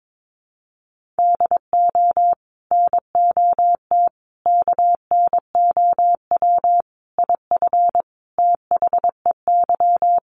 Morse code, heard as a keyed tone, about 22 wpm, 700 Hz. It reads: DO NOT KNOW IF THEY
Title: Sets of 5 Words Encore 22wpm